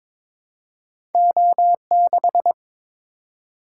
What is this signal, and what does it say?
Morse code, heard as a keyed tone, about 22 wpm, 700 Hz. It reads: O6
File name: Two Letter-Number Combinations - 22wpm